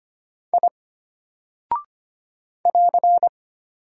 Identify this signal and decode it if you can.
Morse code, heard as a keyed tone, about 25 wpm, 700 Hz. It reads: I <AR>
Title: Single Character (Letter:Number:Punctuation:Prosign) Alt 25wpm